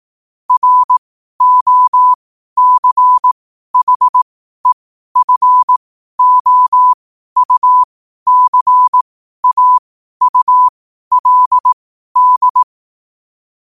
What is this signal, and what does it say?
Morse code, heard as a keyed tone, about 18 wpm, spaced as for 12 wpm, 1000 Hz. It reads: ROCHEFOUCAULD
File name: qotdfiles / 12wpm